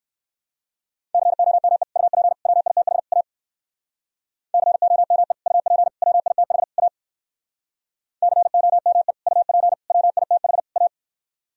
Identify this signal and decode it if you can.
Morse code, heard as a keyed tone, about 56 wpm, 700 Hz. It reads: COME UP WITH A COME UP WITH A COME UP WITH A
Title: Speed-Racing - Sets of 4 Words 45wpm